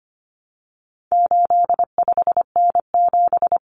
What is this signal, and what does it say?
Morse code, heard as a keyed tone, about 25 wpm, 700 Hz. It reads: 85N7
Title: Common Cut Numbers - 4-Digit 25wpm